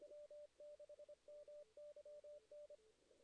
Morse code, heard as a keyed tone, about 25 wpm, 590 Hz. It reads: W6MYN
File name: 2016 JAN CW NAQP